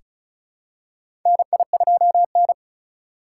Morse code, heard as a keyed tone, about 36 wpm, 700 Hz. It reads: NI2N
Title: USA Call Signs - 4 Characters - 2x Character Spacing - 35wpm